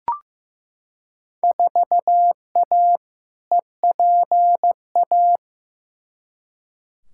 Morse code, heard as a keyed tone, about 15 wpm, 700 Hz. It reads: 4A EPA
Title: Field Day Exchanges - 15wpm